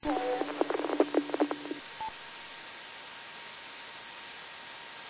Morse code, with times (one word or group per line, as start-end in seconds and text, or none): none